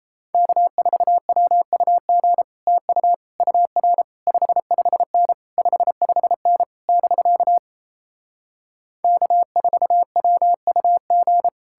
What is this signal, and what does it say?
Morse code, heard as a keyed tone, about 37 wpm, 700 Hz. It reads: K4WUG TU UR 55N 55N <BK> K4WUG